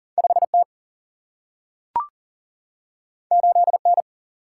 Morse code, heard as a keyed tone, about 42 wpm, 700 Hz. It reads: 5T 8N
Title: Single-Letter Single-Number 40wpm